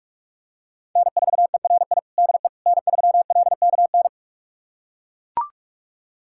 Morse code, heard as a keyed tone, about 42 wpm, 700 Hz. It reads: N4ERI DE N3LKN
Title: USA Callsign DE Callsign - 5 Characters - 45wpm